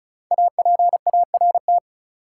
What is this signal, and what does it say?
Morse code, heard as a keyed tone, about 36 wpm, 700 Hz. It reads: APART
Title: Five Letter Words - 35wpm